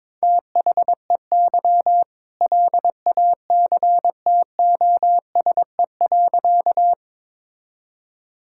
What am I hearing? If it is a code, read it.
Morse code, heard as a keyed tone, about 22 wpm, 700 Hz. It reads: THEY LACTOSE.
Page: morse-code-ninja-practice-22wpm